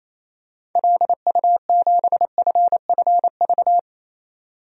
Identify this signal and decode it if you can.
Morse code, heard as a keyed tone, about 28 wpm, 700 Hz. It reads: LU7FFV
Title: DXCC-100 Call Signs - 6 Characters - 28wpm